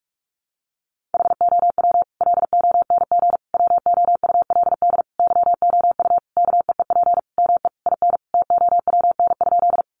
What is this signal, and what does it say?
Morse code, heard as a keyed tone, about 44 wpm, 700 Hz. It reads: HOW LONG WOULD YOU KEEP ME IN TOWN?
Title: Sentences from Top 500 Words 45wpm